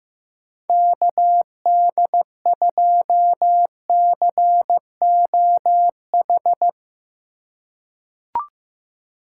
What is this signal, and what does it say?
Morse code, heard as a keyed tone, about 15 wpm, 700 Hz. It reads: KD2COH
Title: USA Call Signs - 6 Characters - 4x Character Spacing - 15wpm